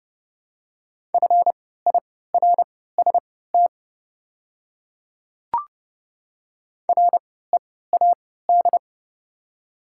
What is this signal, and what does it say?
Morse code, heard as a keyed tone, about 30 wpm, spaced as for 17 wpm, 700 Hz. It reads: FIRST READ
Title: 3x Character Spacing - Top 500 Words - 30wpm